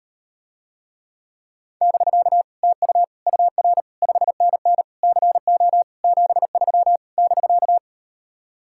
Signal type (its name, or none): Morse code